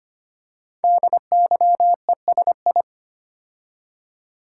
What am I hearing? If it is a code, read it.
Morse code, heard as a keyed tone, about 25 wpm, 700 Hz. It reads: DYESI
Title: Code Groups - 5 Letters - 25wpm